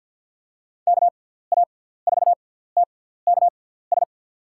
Morse code, heard as a keyed tone, about 53 wpm, spaced as for 17 wpm, 700 Hz. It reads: KA4TKS
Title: USA Call Signs - 6 Characters - 6x Character Spacing - 50wpm